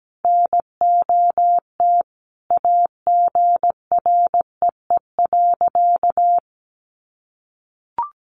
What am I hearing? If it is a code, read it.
Morse code, heard as a keyed tone, about 17 wpm, 700 Hz. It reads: NOT AGREE.
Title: Sentences from Top 1100 Words 17wpm